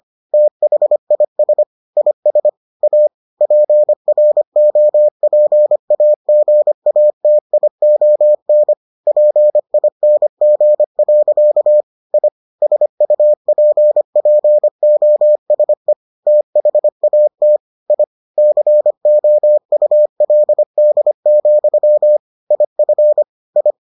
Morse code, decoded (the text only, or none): THIS IS A PROPAGATION PING. I SUPPOSE THAT I COULD, IF I